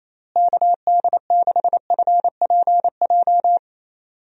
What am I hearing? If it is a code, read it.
Morse code, heard as a keyed tone, about 27 wpm, 700 Hz. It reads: KD6FPJ
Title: USA Call Signs - 6 Characters - Repeated 3x - 28wpm